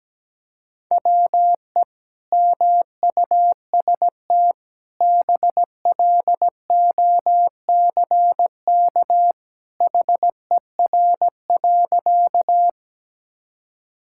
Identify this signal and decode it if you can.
Morse code, heard as a keyed tone, about 17 wpm, 700 Hz. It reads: WE MUST BLOCK HER.
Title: Sentences from Top 1600 Words 17wpm